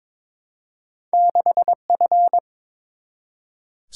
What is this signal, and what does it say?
Morse code, heard as a keyed tone, about 22 wpm, 700 Hz. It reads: 6F
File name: practice / 22wpm / Two Letter-Number Combinations - 22wpm